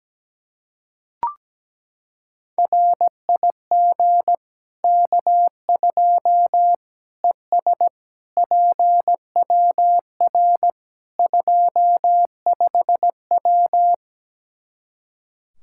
Morse code, heard as a keyed tone, about 17 wpm, 700 Hz. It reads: RIG K2 ES PWR 25W
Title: Practice QSO - Exchange 2 - Info - 17wpm